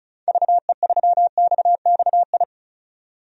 Morse code, heard as a keyed tone, about 36 wpm, 700 Hz. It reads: VE3XXI
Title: DXCC-100 Call Signs - 6 Characters - 35wpm